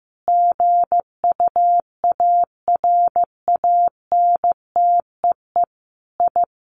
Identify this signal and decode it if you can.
Morse code, heard as a keyed tone, about 15 wpm, 700 Hz. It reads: GUARANTEE I
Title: A Sound Of Thunder 15wpm